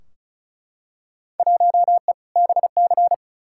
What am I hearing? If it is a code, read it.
Morse code, heard as a keyed tone, about 34 wpm, 700 Hz. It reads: 1E BC
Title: Field Day Exchanges - 35wpm